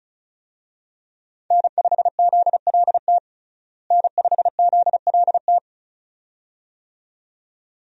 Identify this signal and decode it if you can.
Morse code, heard as a keyed tone, about 34 wpm, 700 Hz. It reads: N5ZLT N5ZLT
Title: USA Call Signs - 5 Characters - Repeated 2x - 35wpm